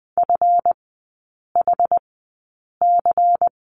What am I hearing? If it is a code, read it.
Morse code, heard as a keyed tone, about 20 wpm, spaced as for 8 wpm, 700 Hz. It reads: FHC
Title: 166 Lesson 27e - Number 7 - Three Characters 20wpm